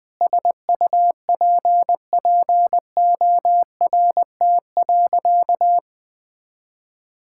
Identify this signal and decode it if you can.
Morse code, heard as a keyed tone, about 20 wpm, 700 Hz. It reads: SUPPORT.
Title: Sentences from Top 1000 Words - Review Part 5 - 20wpm